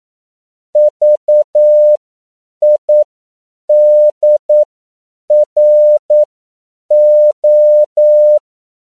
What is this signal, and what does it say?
Morse code, heard as a keyed tone, about 9 wpm, spaced as for 7 wpm, 600 Hz. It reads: VIDRO
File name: CW / 10PPM